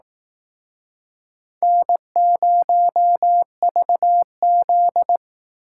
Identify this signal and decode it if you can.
Morse code, heard as a keyed tone, about 19 wpm, 700 Hz. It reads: N0VZ
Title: Speed Racing - USA Call Signs 15wpm